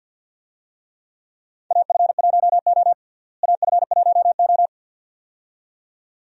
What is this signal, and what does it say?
Morse code, heard as a keyed tone, about 50 wpm, 700 Hz. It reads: AF1O AF1O
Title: USA Call Signs - 4 Characters - Repeated 2x - 50wpm